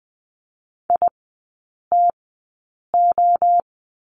Morse code, heard as a keyed tone, about 20 wpm, 700 Hz. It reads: I T O
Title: 029 Lesson 6e - Letter i - Three Characters 20wpm